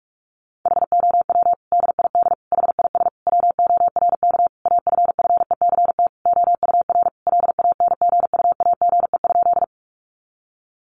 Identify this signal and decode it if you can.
Morse code, heard as a keyed tone, about 44 wpm, 700 Hz. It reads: HOW DID HIS WORK AFFECT OUR LANGUAGE?
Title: Sentences from Top 900 Words 45wpm